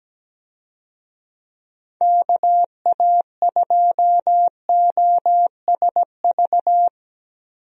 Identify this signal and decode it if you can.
Morse code, heard as a keyed tone, about 17 wpm, 700 Hz. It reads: KA2OSV
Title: USA Call Signs - 6 Characters - Repeated 2x - 17wpm